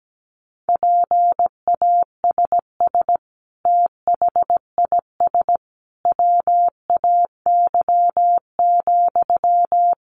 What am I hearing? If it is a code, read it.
Morse code, heard as a keyed tone, about 17 wpm, 700 Hz. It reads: PASS THIS WAY,
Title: Sentences from Top 400 Words 17wpm